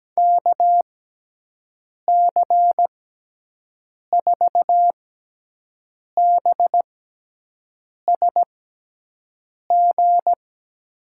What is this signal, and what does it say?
Morse code, heard as a keyed tone, about 17 wpm, spaced as for 6 wpm, 700 Hz. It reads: KC4BSG
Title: USA Call Signs - 6 Characters - 6x Character Spacing - 17wpm